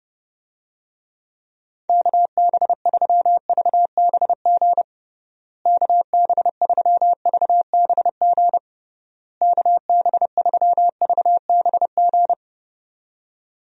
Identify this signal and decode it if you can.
Morse code, heard as a keyed tone, about 32 wpm, 700 Hz. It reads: KB3VBG KB3VBG KB3VBG
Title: USA Call Signs - 6 Characters - Repeated 3x - 30wpm